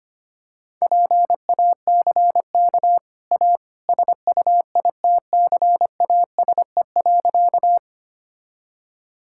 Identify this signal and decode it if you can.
Morse code, heard as a keyed tone, about 25 wpm, 700 Hz. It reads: PACK A SUITCASE.
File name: Easy Sentences with Hints - 25wpm